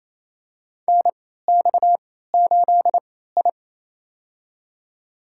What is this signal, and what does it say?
Morse code, heard as a keyed tone, about 27 wpm, spaced as for 16 wpm, 700 Hz. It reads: NX8I
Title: USA Call Signs - 4 Characters - 3x Character Spacing - 28wpm